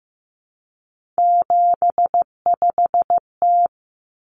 Morse code, heard as a keyed tone, about 15 wpm, 700 Hz. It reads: 75T
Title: Common Cut Numbers - 3-Digit 15wpm